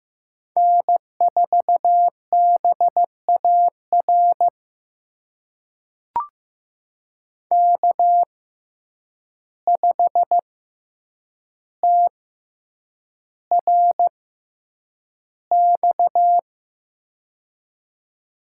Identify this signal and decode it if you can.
Morse code, heard as a keyed tone, about 15 wpm, spaced as for 8 wpm, 700 Hz. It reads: N4BAR K5TRX
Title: USA Call Signs - 5 Characters - 6x Character Spacing - 15wpm